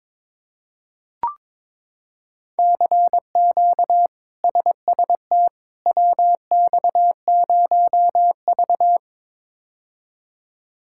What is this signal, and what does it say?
Morse code, heard as a keyed tone, about 22 wpm, 700 Hz. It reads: CQ SST WX0V
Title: K1USN SST QSO Practice - 22wpm